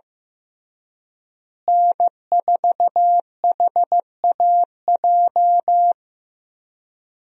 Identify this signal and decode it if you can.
Morse code, heard as a keyed tone, about 15 wpm, 700 Hz. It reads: N4HAJ
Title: USA Call Signs - 5 Characters - Repeated 3x - 15wpm